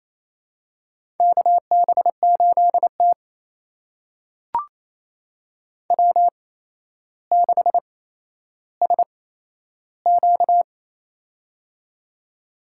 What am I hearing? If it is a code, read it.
Morse code, heard as a keyed tone, about 28 wpm, spaced as for 12 wpm, 700 Hz. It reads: KB8T W6SQ
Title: USA Call Signs - 4 Characters - 8x Character Spacing - 28wpm